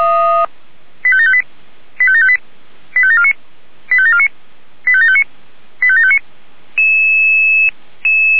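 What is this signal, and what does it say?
Five-tone selective call (ZVEI), heard as quick runs of five tones, tones of about 70 ms. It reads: 85758 85758 85749 85749 85759 85759
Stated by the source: ZVEI five-tone selective call received with an SDR.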